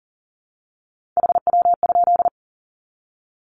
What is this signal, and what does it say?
Morse code, heard as a keyed tone, about 40 wpm, 700 Hz. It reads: HW?
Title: Common QSO Phrases 40wpm